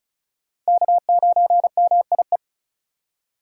Morse code, heard as a keyed tone, about 36 wpm, 700 Hz. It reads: K9MIE
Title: USA Call Signs - 5 Characters - 2x Character Spacing - 35wpm